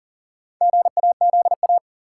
Morse code, heard as a keyed tone, about 39 wpm, 700 Hz. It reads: GAZA
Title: Sets of 5 Words - New Word Drill - 40wpm